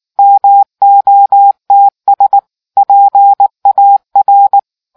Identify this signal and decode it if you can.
Morse code, heard as a keyed tone, about 19 wpm, 800 Hz. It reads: MOTS PAR